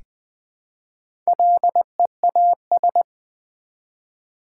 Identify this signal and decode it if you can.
Morse code, heard as a keyed tone, about 20 wpm, 700 Hz. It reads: LEAS
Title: Build A Word - 1 Letter At A Time - 3 to 5-Letter Words - 20wpm